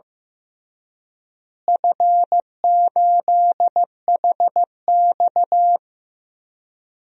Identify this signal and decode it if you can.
Morse code, heard as a keyed tone, about 15 wpm, 700 Hz. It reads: F8HX
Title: DXCC-100 Call Signs - 4 Characters - Repeated 2x - 15wpm